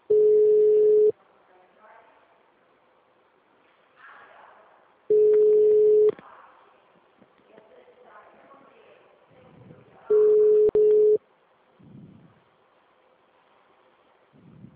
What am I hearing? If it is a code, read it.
Ringback tone (Europe, 425 Hz), with speech.